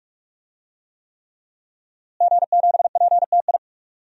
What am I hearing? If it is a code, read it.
Morse code, heard as a keyed tone, about 41 wpm, 700 Hz. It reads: G7PTI
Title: DXCC-100 Call Signs - 5 Characters - Repeated 3x - 45wpm